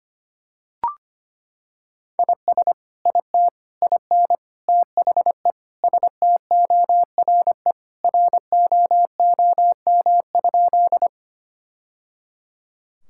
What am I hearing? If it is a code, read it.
Morse code, heard as a keyed tone, about 25 wpm, 700 Hz. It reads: IS IT IN THE STORE ROOM?
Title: Sentences from Top 1000 Words - Review Part 5 - 25wpm